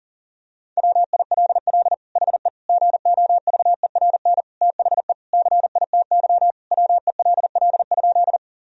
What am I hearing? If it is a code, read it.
Morse code, heard as a keyed tone, about 40 wpm, 700 Hz. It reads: WILL HE GOVERN THE CITY WELL?
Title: Sentences from Top 1000 Words - Review Part 2 - 40wpm